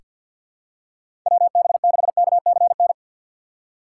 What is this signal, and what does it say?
Morse code, heard as a keyed tone, about 46 wpm, 700 Hz. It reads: WB6KCN
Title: USA Call Signs - 6 Characters - 6x Character Spacing - 50wpm